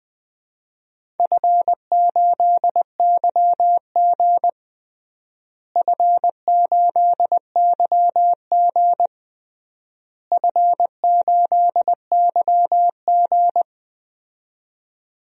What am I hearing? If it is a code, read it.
Morse code, heard as a keyed tone, about 20 wpm, 700 Hz. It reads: F8YG F8YG F8YG
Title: DXCC-100 Call Signs - 4 Characters - Repeated 3x - 20wpm